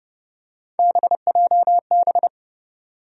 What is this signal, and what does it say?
Morse code, heard as a keyed tone, about 30 wpm, 700 Hz. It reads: BJB
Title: USA Call Sign Postfixes - 3 Letters 30wpm